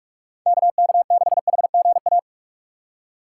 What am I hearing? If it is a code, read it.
Morse code, heard as a keyed tone, about 45 wpm, 700 Hz. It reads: KK6HGA
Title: USA Call Signs - 6 Characters - 6x Character Spacing - 45wpm